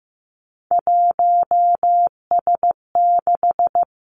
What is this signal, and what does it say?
Morse code, heard as a keyed tone, about 15 wpm, 700 Hz. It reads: 1S6
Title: Three Challenging Letter-Number Combinations 15wpm